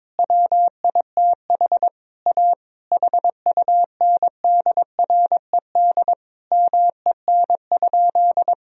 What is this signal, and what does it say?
Morse code, heard as a keyed tone, about 22 wpm, 700 Hz. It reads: WITH A HUNDRED MEN?
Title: Sentences from Top 1000 Words - Review Part 2 - 22wpm